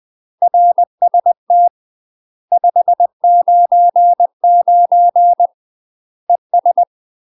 Morse code, heard as a keyed tone, about 20 wpm, 700 Hz. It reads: RST 599 ES
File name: UK MASTER QSOs TRAINING / 20WPM / UK MASTER QSOs 0029